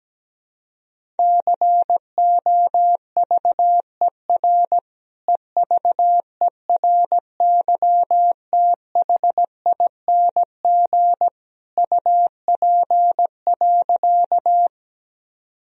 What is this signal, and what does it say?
Morse code, heard as a keyed tone, about 17 wpm, 700 Hz. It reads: COVER EVERYTHING UP.
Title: Sentences from Top 1000 Words - Review Part 1 - 17wpm